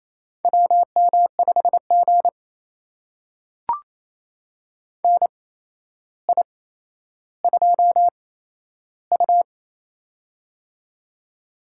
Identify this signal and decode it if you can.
Morse code, heard as a keyed tone, about 28 wpm, spaced as for 12 wpm, 700 Hz. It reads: WM5G NI2U
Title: USA Call Signs - 4 Characters - 8x Character Spacing - 28wpm